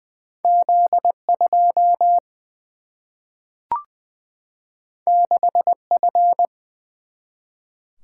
Morse code, heard as a keyed tone, about 20 wpm, 700 Hz. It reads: Z 2 6 F
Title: Single-Letter Single-Number 20wpm